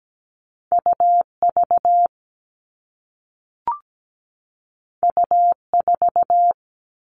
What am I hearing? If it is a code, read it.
Morse code, heard as a keyed tone, about 17 wpm, 700 Hz. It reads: UV U4
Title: All Cut Numbers - 2-Digit 17wpm